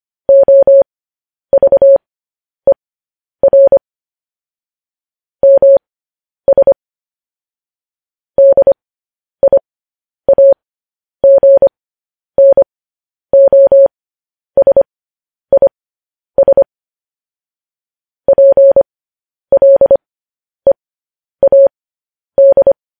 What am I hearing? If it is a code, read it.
Morse code, heard as a keyed tone, about 25 wpm, spaced as for 10 wpm, 550 Hz. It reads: OVER MS DIAGNOSIS PLEAD
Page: news headlines in morse code at 10 wpm